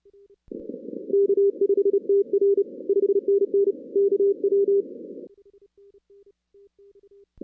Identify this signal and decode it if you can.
Morse code, heard as a keyed tone, about 29 wpm, 410 Hz. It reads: K5TR 5NN KW
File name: 2016 arrl dx cw / 2016-ARRL-DX-CW-K5TR- - 0-46